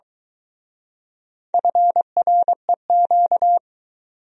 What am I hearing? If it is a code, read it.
Morse code, heard as a keyed tone, about 22 wpm, 700 Hz. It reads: FREQ